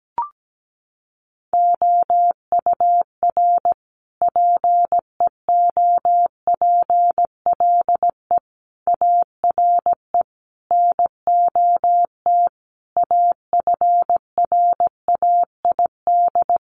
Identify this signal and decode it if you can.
Morse code, heard as a keyed tone, about 17 wpm, 700 Hz. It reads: OUR PEOPLE ARE NOT AFRAID
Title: Sentences from Top 1500 Words 17wpm